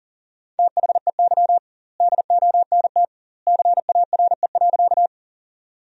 Morse code, heard as a keyed tone, about 37 wpm, 700 Hz. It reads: THEY DONT CARE.
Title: 2x Word-Spacing - Sentences from Top 300 Words - 40wpm